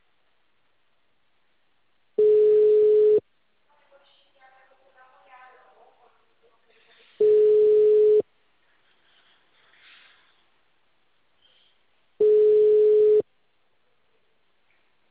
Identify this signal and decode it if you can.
Ringback tone (Europe, 425 Hz), with speech.